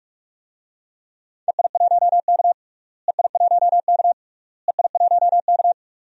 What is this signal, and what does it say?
Morse code, heard as a keyed tone, about 41 wpm, 700 Hz. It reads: EI1K EI1K EI1K
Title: DXCC-100 Call Signs - 4 Characters - Repeated 3x - 45wpm